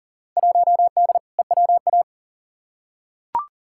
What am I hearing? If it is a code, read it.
Morse code, heard as a keyed tone, about 40 wpm, 700 Hz. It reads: 1D EWA
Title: Field Day Exchanges - 40wpm